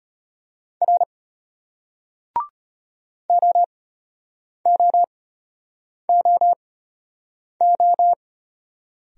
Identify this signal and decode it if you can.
Morse code, heard as a keyed tone, about 33 wpm, spaced as for 8 wpm, 700 Hz. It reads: R OOOO